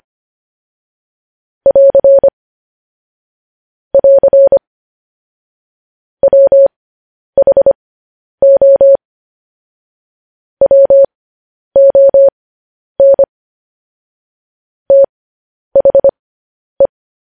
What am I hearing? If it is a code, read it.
Morse code, heard as a keyed tone, about 25 wpm, spaced as for 10 wpm, 550 Hz. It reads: <AR> <AR> WHO WON THE
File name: podcast / News headlines-1x-25x10